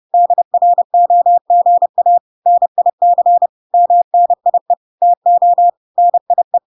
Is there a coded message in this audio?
Yes: Morse code from 0.1 to 6.6 s, heard as a keyed tone, about 30 wpm, 700 Hz. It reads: DROGA NIC MNIE TO NIE